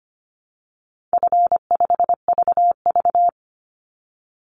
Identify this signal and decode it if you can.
Morse code, heard as a keyed tone, about 25 wpm, 700 Hz. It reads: F5VV
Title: Easy Call Signs - 25wpm